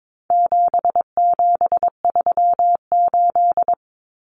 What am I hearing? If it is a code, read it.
Morse code, heard as a keyed tone, about 22 wpm, 700 Hz. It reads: 7738
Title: Numbers - 4-Digit 22wpm